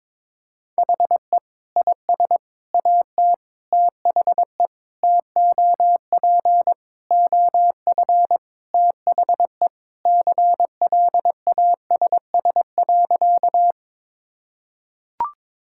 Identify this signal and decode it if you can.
Morse code, heard as a keyed tone, about 21 wpm, 700 Hz. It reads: HE IS AT THE TOP OF THE CLASS.
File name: practice / 22wpm / Sentences with 3-Word Phrases - 22wpm